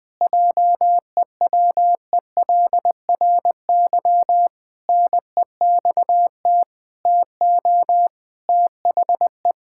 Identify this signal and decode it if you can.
Morse code, heard as a keyed tone, about 20 wpm, 700 Hz. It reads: JEWELRY NEXT TO THE
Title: Pangram Challenge v2 - 20wpm